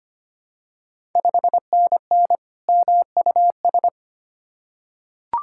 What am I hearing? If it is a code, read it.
Morse code, heard as a keyed tone, about 25 wpm, 700 Hz. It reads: 5NN MUS
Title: Ontario QSO Party Exchange 25wpm